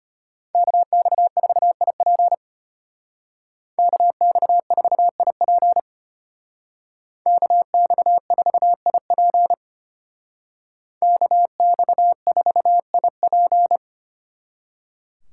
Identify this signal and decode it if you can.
Morse code, heard as a keyed tone, about 33 wpm, 700 Hz. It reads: KX4IP KX4IP KX4IP KX4IP